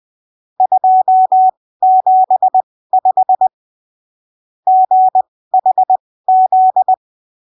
Morse code, heard as a keyed tone, about 20 wpm, spaced as for 15 wpm, 750 Hz. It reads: 275 GHZ
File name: arquivos / texto / texto 12 20ppm